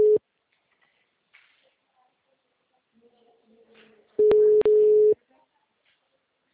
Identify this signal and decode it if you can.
Ringback tone (Europe, 425 Hz).